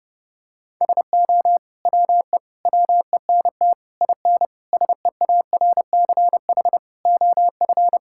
Morse code, heard as a keyed tone, about 32 wpm, 700 Hz. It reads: SO WE WENT IN SEARCH OF
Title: Sentences from Top 1000 Words - Review Part 5 - 30wpm